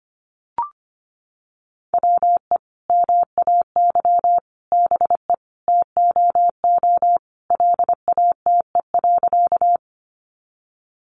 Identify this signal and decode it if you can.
Morse code, heard as a keyed tone, about 25 wpm, 700 Hz. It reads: WE MAY BE TOO LATE.
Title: Sentences from Top 300 Words 25wpm